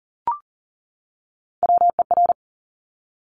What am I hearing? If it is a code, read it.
Morse code, heard as a keyed tone, about 42 wpm, 690 Hz. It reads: WER
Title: Top Three Letter Combinations in Words 40wpm